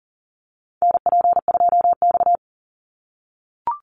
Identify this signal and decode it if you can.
Morse code, heard as a keyed tone, about 38 wpm, 700 Hz. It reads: NP2X
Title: Call Signs - 40wpm